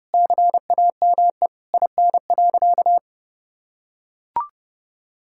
Morse code, heard as a keyed tone, about 32 wpm, 700 Hz. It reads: CAME IN.
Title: Sentences from Top 1000 Words - Review Part 5 - 30wpm